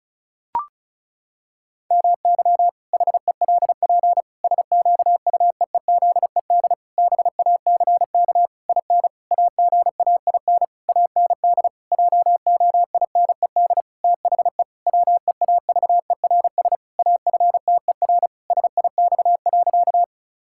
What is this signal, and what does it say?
Morse code, heard as a keyed tone, about 34 wpm, 700 Hz. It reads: MY HELP SQUEEZED BACK IN AGAIN AND JOINED THE WEAVERS AFTER SIX.
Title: Pangram Challenge v2 - 35wpm